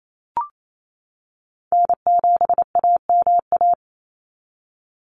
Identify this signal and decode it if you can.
Morse code, heard as a keyed tone, about 28 wpm, 700 Hz. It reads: N7AMA
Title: NA SOTA Chaser Callsigns 28wpm